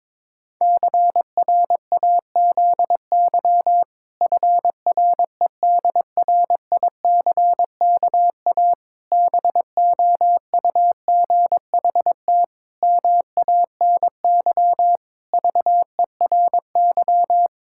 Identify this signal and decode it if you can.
Morse code, heard as a keyed tone, about 22 wpm, 700 Hz. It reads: CRAZY FREDRICKA BOUGHT MANY VERY
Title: Pangram Challenge v2 - 22wpm